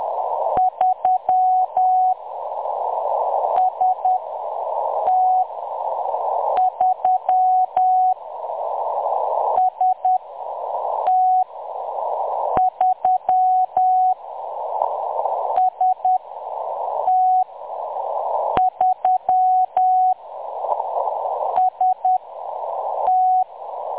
Morse code, heard as a keyed tone, about 10 wpm, spaced as for 6 wpm, 740 Hz. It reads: 3 ST3 ST3 ST3 ST
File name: M23-ST3- -0858utc-10184khz